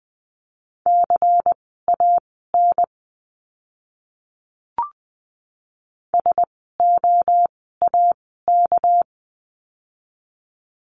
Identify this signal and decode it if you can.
Morse code, heard as a keyed tone, about 20 wpm, spaced as for 14 wpm, 700 Hz. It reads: CAN SOAK
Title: 199 Lesson 32f - Letter k - Words 20wpm